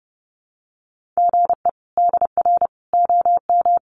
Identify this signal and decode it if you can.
Morse code, heard as a keyed tone, about 30 wpm, 700 Hz. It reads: GE DR OM
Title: Common QSO Phrases 30wpm